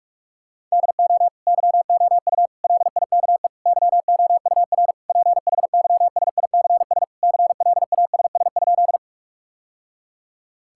Morse code, heard as a keyed tone, about 41 wpm, 700 Hz. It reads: DO YOU LIKE YOUR PHYSICS CLASS?
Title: Sentences from Top 1000 Words - Review Part 5 - 45wpm